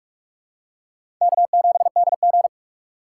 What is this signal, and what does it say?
Morse code, heard as a keyed tone, about 42 wpm, 700 Hz. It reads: K7DG
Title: USA Call Signs - 4 Characters - 3x Character Spacing - 45wpm